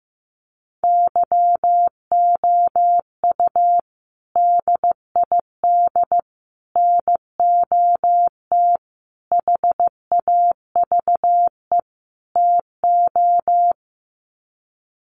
Morse code, heard as a keyed tone, about 15 wpm, 700 Hz. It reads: YOU DID NOT HAVE TO
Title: Sets of 5 Words Encore 15wpm